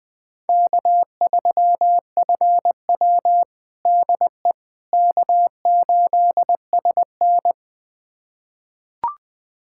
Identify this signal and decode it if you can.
Morse code, heard as a keyed tone, about 20 wpm, 700 Hz. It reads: K3FW DE K8SN
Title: USA Callsign DE Callsign - 4 Characters - 20wpm